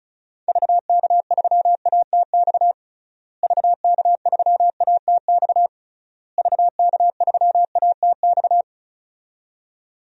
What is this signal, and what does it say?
Morse code, heard as a keyed tone, about 34 wpm, 700 Hz. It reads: VK3ATX VK3ATX VK3ATX
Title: DXCC-100 Call Signs - 6 Characters - Repeated 3x - 35wpm